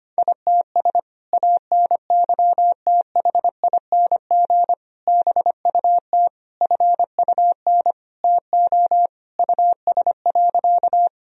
Morse code, heard as a keyed tone, about 25 wpm, 700 Hz. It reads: ITS ANYTHING BUT FUN TO US.
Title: Sentences from Top 1000 Words - Review Part 4 - 25wpm